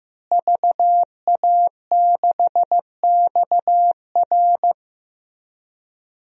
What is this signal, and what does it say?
Morse code, heard as a keyed tone, about 15 wpm, 700 Hz. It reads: VA6XR
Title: DXCC-100 Call Signs - 5 Characters - Repeated 3x - 15wpm